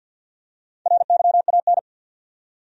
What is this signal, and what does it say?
Morse code, heard as a keyed tone, about 52 wpm, 700 Hz. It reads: RYAN
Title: USA Common Names - Max 5 Letters - 50wpm